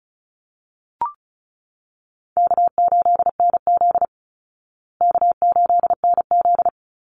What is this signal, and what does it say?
Morse code, heard as a keyed tone, about 34 wpm, 700 Hz. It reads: K8NZ K8NZ
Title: Easy Call Signs Repeated 2x - 35wpm